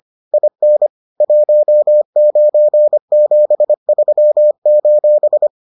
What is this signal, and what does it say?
Morse code, heard as a keyed tone, about 25 wpm, 600 Hz. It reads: IN 1973: